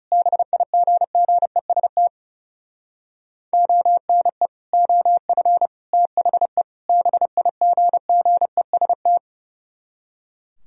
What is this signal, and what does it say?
Morse code, heard as a keyed tone, about 40 wpm, 700 Hz. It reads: BIGGEST ONE OF THE BIGGEST